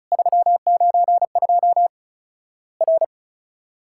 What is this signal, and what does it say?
Morse code, heard as a keyed tone, about 34 wpm, 700 Hz. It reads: 392 R
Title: Worked All Europe DX Contest - QTC Exchange - 35wpm